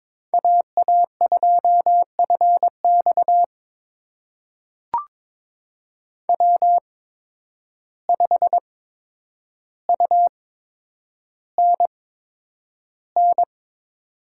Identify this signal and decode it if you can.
Morse code, heard as a keyed tone, about 22 wpm, spaced as for 9 wpm, 700 Hz. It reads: AA2FX W5UNN
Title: USA Call Signs - 5 Characters - 8x Character Spacing - 22wpm